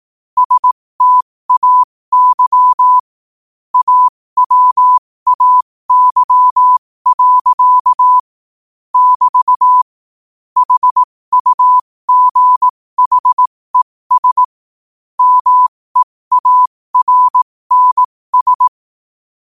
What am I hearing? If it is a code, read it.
Morse code, heard as a keyed tone, about 18 wpm, spaced as for 15 wpm, 1000 Hz. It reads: STAY AWAY. <BT> HUGHES MEARNS
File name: qotdfiles / 15wpm